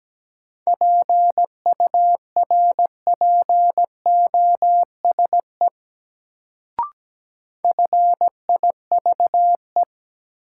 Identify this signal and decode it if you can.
Morse code, heard as a keyed tone, about 17 wpm, 700 Hz. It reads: PURPOSE FIVE
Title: Sets of 5 Words - New Word Drill - 17wpm